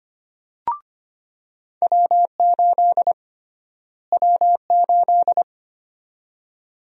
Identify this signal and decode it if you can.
Morse code, heard as a keyed tone, about 25 wpm, 700 Hz. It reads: W8 W8
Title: USA Call Sign Prefixes - Repeated 2x - 25wpm